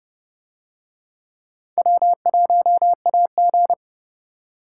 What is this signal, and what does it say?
Morse code, heard as a keyed tone, about 27 wpm, 700 Hz. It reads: W1AG